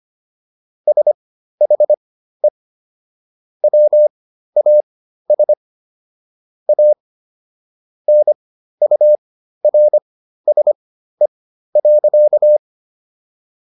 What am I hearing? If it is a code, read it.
Morse code, heard as a keyed tone, about 25 wpm, spaced as for 13 wpm, 600 Hz. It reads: SHE WAS A NURSE.